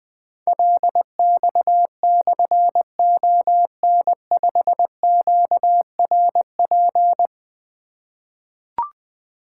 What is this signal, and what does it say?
Morse code, heard as a keyed tone, about 20 wpm, 700 Hz. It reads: LX/ON5QRP
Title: Difficult Call Signs 20wpm